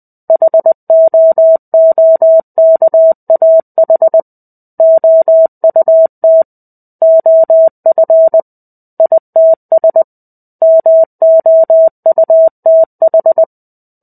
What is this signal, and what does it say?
Morse code, heard as a keyed tone, about 20 wpm, 650 Hz. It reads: HOOKAH OUT OF ITS MOUTH
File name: 05-Chapter-05 20 WPM